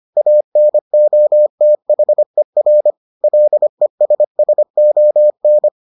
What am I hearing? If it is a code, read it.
Morse code, heard as a keyed tone, about 25 wpm, 600 Hz. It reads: ANOTHER LESSON